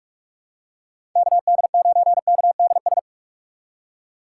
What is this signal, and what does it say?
Morse code, heard as a keyed tone, about 46 wpm, 700 Hz. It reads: KD9KDS
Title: USA Call Signs - 6 Characters - 4x Character Spacing - 45wpm